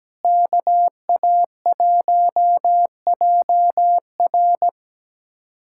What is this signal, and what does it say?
Morse code, heard as a keyed tone, about 17 wpm, 700 Hz. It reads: KA1JR
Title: USA Callsign DE Callsign - 5 Characters - 17wpm